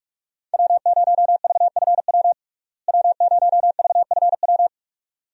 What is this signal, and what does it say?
Morse code, heard as a keyed tone, about 46 wpm, 700 Hz. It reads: W0VFW W0VFW
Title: USA Call Signs - 5 Characters - Repeated 2x - 45wpm